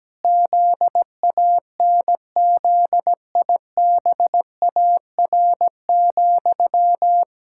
Morse code, heard as a keyed tone, about 17 wpm, 700 Hz. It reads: ZANZIBAR,
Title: Pangram Challenge v2 - 17wpm